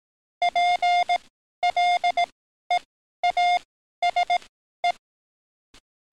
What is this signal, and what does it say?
Morse code, heard as a keyed tone, about 18 wpm, spaced as for 12 wpm, 710 Hz. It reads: PLEASE
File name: training / NAVPERS 11049to11060 / NAVPERS 11053 RA RB